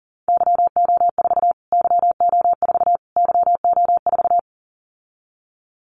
Morse code, heard as a keyed tone, about 40 wpm, 700 Hz. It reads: YO4 YO4 YO4
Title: Callsign Prefixes for Top 25 DXCC - Repeated 3x - 40wpm